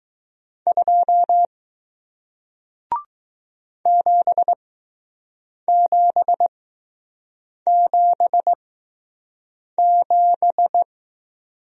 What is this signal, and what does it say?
Morse code, heard as a keyed tone, about 20 wpm, spaced as for 6 wpm, 700 Hz. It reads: 2 7777